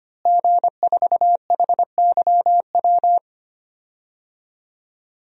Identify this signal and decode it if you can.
Morse code, heard as a keyed tone, about 25 wpm, 700 Hz. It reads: G4HYW
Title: DXCC-100 Call Signs - 5 Characters - Repeated 3x - 25wpm